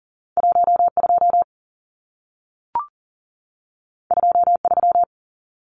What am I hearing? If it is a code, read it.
Morse code, heard as a keyed tone, about 40 wpm, 700 Hz. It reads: 12 23
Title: Numbers - 2-Digit 40wpm